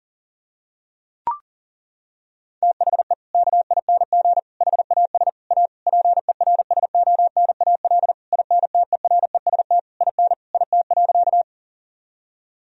Morse code, heard as a keyed tone, about 40 wpm, 690 Hz. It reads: THE KING HAS A PERSONAL INTEREST IN IT.
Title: Sentences from Top 1000 Words - Review Part 4 - 40wpm